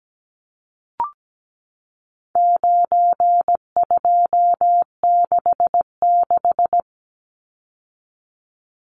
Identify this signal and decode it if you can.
Morse code, heard as a keyed tone, about 17 wpm, 700 Hz. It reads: 9266
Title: Numbers - 4-Digit 17wpm